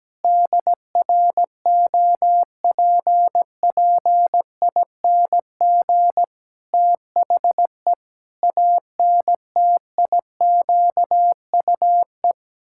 Morse code, heard as a keyed tone, about 17 wpm, 700 Hz. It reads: DROPPING THE ANTIQUE
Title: Pangram Challenge v2 - 17wpm